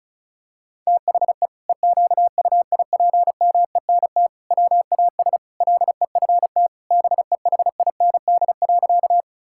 Morse code, heard as a keyed tone, about 34 wpm, 700 Hz. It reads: THE EQUIPMENT WAS LEFT BEHIND.
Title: Sentences from Top 1000 Words - Review Part 5 - 35wpm